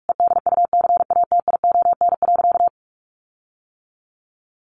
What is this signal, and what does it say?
Morse code, heard as a keyed tone, about 44 wpm, 700 Hz. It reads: EDUCATION.
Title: Sentences from Top 1500 Words 45wpm